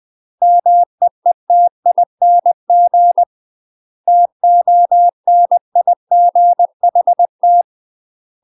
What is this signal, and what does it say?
Morse code, heard as a keyed tone, about 20 wpm, 700 Hz. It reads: MEETING TONIGHT
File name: UK MASTER QSOs TRAINING / 20WPM / UK MASTER QSOs 0073